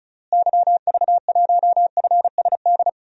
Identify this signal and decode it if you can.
Morse code, heard as a keyed tone, about 34 wpm, 700 Hz. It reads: YV1FSD
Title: DXCC-100 Call Signs - 6 Characters - Repeated 2x - 35wpm